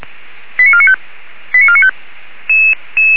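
Five-tone selective call (ZVEI), heard as quick runs of five tones, tones of about 80 ms. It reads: 89495 89495
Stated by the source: ZVEI five-tone call 85371 received by SDR, 464 channel.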